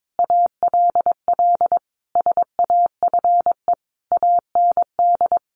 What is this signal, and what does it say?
Morse code, heard as a keyed tone, about 22 wpm, 700 Hz. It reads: ALL SAFE AND
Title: Sentences from Top 900 Words 22wpm